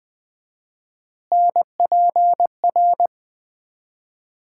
Morse code, heard as a keyed tone, about 20 wpm, 700 Hz. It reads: NPR
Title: 4x Word-Spacing - Sets of 5 Words - 20wpm